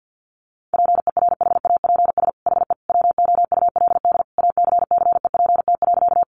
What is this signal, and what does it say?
Morse code, heard as a keyed tone, about 50 wpm, 700 Hz. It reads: PERHAPS HE WOULD ACCEPT.
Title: Sentences from Top 1300 Words 50wpm